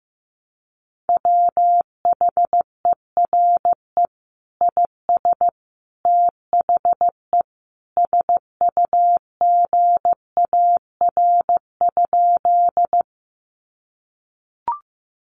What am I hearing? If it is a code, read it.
Morse code, heard as a keyed tone, about 15 wpm, 700 Hz. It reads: WHERE IS THE SUGAR?
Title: Sentences from Top 1200 Words 15wpm